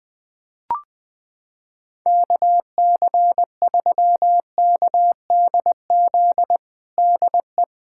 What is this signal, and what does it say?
Morse code, heard as a keyed tone, about 20 wpm, 700 Hz. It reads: KC3KDZ DE
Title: USA Callsign DE Callsign - 6 Characters - 20wpm